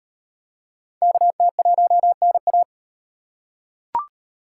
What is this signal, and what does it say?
Morse code, heard as a keyed tone, about 36 wpm, 700 Hz. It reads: KT1NA